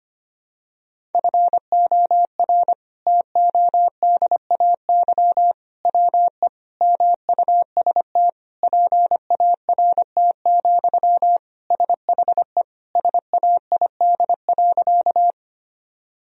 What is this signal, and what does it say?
Morse code, heard as a keyed tone, about 25 wpm, 700 Hz. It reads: FOR TODAY WE MUST PART, SHE SAID.
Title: Sentences from Top 1000 Words - Review Part 1 - 25wpm